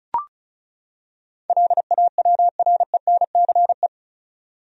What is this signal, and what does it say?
Morse code, heard as a keyed tone, about 34 wpm, 700 Hz. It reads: LAWRENCE
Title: USA Common Names - 35wpm